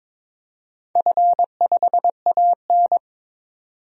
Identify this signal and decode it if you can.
Morse code, heard as a keyed tone, about 22 wpm, 700 Hz. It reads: F5AN
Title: DXCC-100 Call Signs - 4 Characters - 22wpm